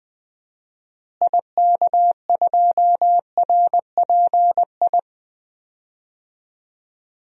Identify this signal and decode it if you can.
Morse code, heard as a keyed tone, about 20 wpm, 700 Hz. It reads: IK2RPI
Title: DXCC-100 Call Signs - 6 Characters - Repeated 2x - 20wpm